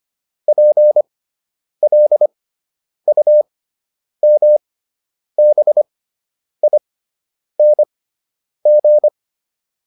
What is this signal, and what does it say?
Morse code, heard as a keyed tone, about 25 wpm, spaced as for 9 wpm, 600 Hz. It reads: PLUMBING
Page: fundamental-practice-files